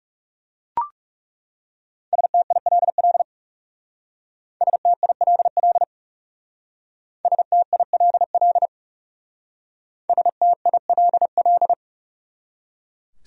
Morse code, heard as a keyed tone, about 39 wpm, 700 Hz. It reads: STILL STILL STILL STILL